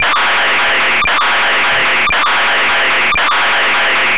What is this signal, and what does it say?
SSTV (slow-scan television), Scottie DX.